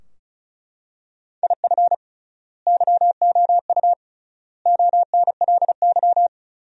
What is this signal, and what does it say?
Morse code, heard as a keyed tone, about 36 wpm, 700 Hz. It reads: IF YOU ONLY
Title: 3x Word-Spacing - Sentences from Top 400 Words - 35wpm